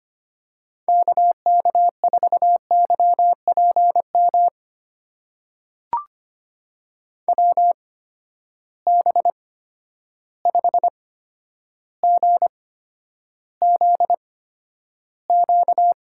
Morse code, heard as a keyed tone, about 25 wpm, spaced as for 11 wpm, 700 Hz. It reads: KK4YPM WB5GZQ
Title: USA Call Signs - 6 Characters - 8x Character Spacing - 25wpm